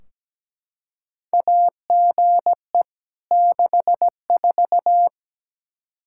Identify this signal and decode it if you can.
Morse code, heard as a keyed tone, about 17 wpm, 700 Hz. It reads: AGE 64
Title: Practice QSO - Exchange 3 - More Info - 17wpm